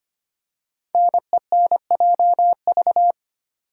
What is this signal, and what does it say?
Morse code, heard as a keyed tone, about 25 wpm, 700 Hz. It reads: NENJV
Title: Code Groups - 5 Letters - 25wpm